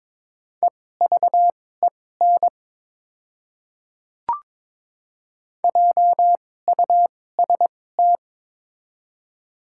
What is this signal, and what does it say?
Morse code, heard as a keyed tone, about 22 wpm, spaced as for 16 wpm, 700 Hz. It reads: EVEN JUST
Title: 2x Character Spacing - Top 200 Words - 22wpm